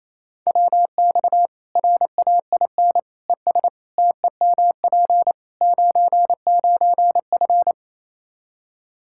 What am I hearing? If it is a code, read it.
Morse code, heard as a keyed tone, about 28 wpm, 700 Hz. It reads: WX RAIN ES TEMP 99F
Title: Practice QSO - Exchange 2 - Info - 28wpm